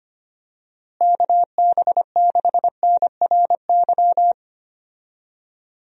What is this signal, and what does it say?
Morse code, heard as a keyed tone, about 25 wpm, 700 Hz. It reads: KB6NRY
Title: USA Call Signs - 6 Characters - Repeated 3x - 25wpm